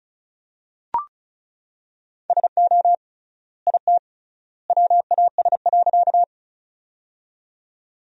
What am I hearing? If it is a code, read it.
Morse code, heard as a keyed tone, about 36 wpm, 700 Hz. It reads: SO IT WAS.
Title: 3x Word-Spacing - Sentences from Top 100 Words - 35wpm